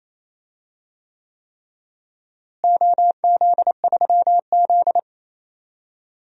Morse code, heard as a keyed tone, about 27 wpm, 700 Hz. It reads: OZ3Z
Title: DXCC-100 Call Signs - 4 Characters - Repeated 3x - 28wpm